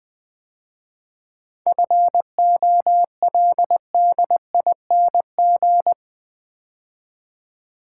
Morse code, heard as a keyed tone, about 20 wpm, 700 Hz. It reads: FOLDING
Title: 7 Letter Words - First 3 Letters Given - 20wpm